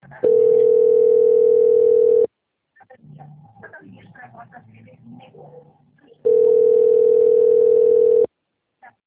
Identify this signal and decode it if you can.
Ringback tone (North America), 440 + 480 Hz, with speech.
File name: out- - - - - .29566